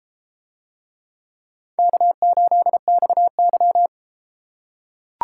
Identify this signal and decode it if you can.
Morse code, heard as a keyed tone, about 31 wpm, 700 Hz. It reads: K8XY